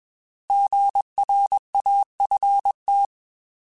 Morse code, heard as a keyed tone, about 21 wpm, 800 Hz. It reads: GRAFT
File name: Alphabetical / Alphabetical mp3s / Gs @20